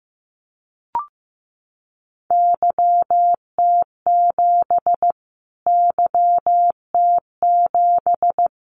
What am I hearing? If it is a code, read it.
Morse code, heard as a keyed tone, about 15 wpm, 700 Hz. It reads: YT7 YT7
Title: Callsign Prefixes for Top 25 DXCC - Repeated 3x - 15wpm